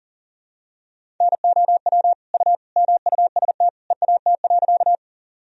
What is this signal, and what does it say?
Morse code, heard as a keyed tone, about 40 wpm, 700 Hz. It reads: NOW U MUST EAT.
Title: Sentences from Top 300 Words and Abbreviations 40wpm